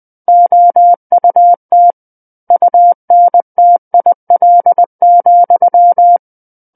Morse code, heard as a keyed tone, about 20 wpm, 700 Hz. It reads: OUT UNTIL,
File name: 20 WPM